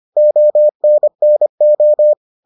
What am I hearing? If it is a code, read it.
Morse code, heard as a keyed tone, about 25 wpm, 600 Hz. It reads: ONNO